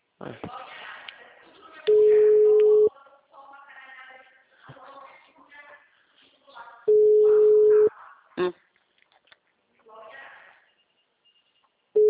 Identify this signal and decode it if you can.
Ringback tone (Europe, 425 Hz), with speech.